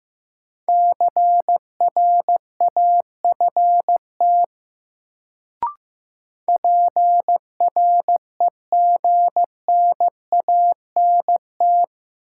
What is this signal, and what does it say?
Morse code, heard as a keyed tone, about 15 wpm, 700 Hz. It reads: CRAFT PREGNANT
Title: Sentences from Top 1000 Words - Review Part 6 - New Word Drill - 15wpm